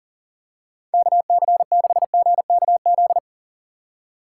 Morse code, heard as a keyed tone, about 38 wpm, 700 Hz. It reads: KC6GKZ
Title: USA Call Signs - 6 Characters - 2x Character Spacing - 40wpm